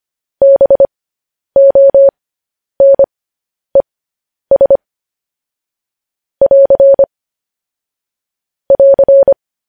Morse code, heard as a keyed tone, about 25 wpm, spaced as for 10 wpm, 550 Hz. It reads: BONES <AR> <AR>
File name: podcast / News headlines-1x-25x10